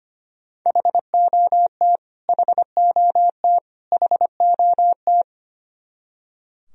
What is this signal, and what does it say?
Morse code, heard as a keyed tone, about 25 wpm, 700 Hz. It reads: HOT HOT HOT
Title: Top 100 Words Repeated 25wpm